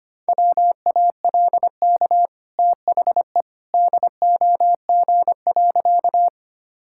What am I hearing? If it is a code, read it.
Morse code, heard as a keyed tone, about 25 wpm, 700 Hz. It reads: WALK THE DOG.
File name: Easy Sentences with Hints - 25wpm